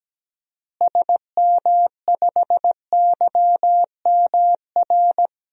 Morse code, heard as a keyed tone, about 17 wpm, 700 Hz. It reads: SM5YMR
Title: DXCC-100 Call Signs - 6 Characters - Repeated 3x - 17wpm